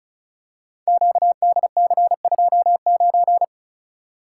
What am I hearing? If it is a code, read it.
Morse code, heard as a keyed tone, about 36 wpm, 700 Hz. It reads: QDC29
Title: Code Groups - 5 Letter:Numbers - 35wpm